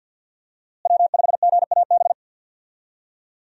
Morse code, heard as a keyed tone, about 45 wpm, 700 Hz. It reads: W5GAB
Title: USA Call Signs - 5 Characters - Repeated 2x - 50wpm